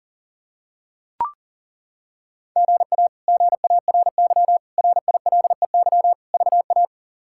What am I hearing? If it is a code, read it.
Morse code, heard as a keyed tone, about 40 wpm, 700 Hz. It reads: GA GARY RILEY VA
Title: K1USN SST QSO Practice - 40wpm